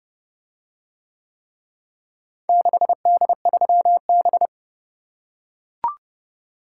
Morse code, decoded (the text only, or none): 6D3B